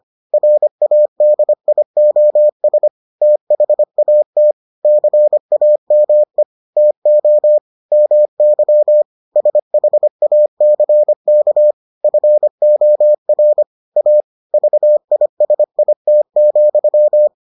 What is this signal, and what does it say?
Morse code, heard as a keyed tone, about 25 wpm, 600 Hz. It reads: RADIOS THAT CAME TO MY SHACK FOR A VISIT,